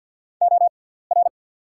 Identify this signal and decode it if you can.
Morse code, heard as a keyed tone, about 54 wpm, spaced as for 17 wpm, 690 Hz. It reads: OR
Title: 6x Character Spacing - Top 100 Words - 50wpm